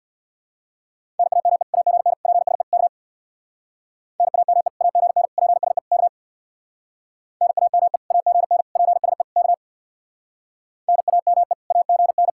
Morse code, heard as a keyed tone, about 63 wpm, 700 Hz. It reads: NAME AGN PSE K NAME AGN PSE K NAME AGN PSE K NAME AGN